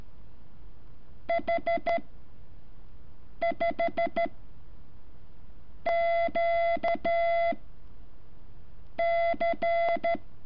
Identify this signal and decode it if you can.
Morse code, heard as a keyed tone, about 13 wpm, 680 Hz. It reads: H 5 Q C
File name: curso / ejercicio 14